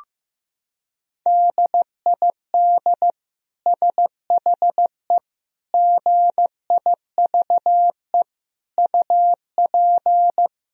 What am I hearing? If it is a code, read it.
Morse code, heard as a keyed tone, about 16 wpm, 700 Hz. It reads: DID SHE GIVE UP
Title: Sentences from Top 1000 Words - Review Part 4 - 15wpm